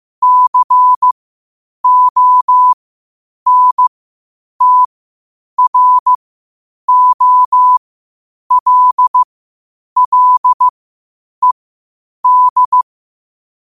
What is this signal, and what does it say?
Morse code, heard as a keyed tone, about 15 wpm, spaced as for 8 wpm, 1000 Hz. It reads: CONTROLLED
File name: qotdfiles / 8wpm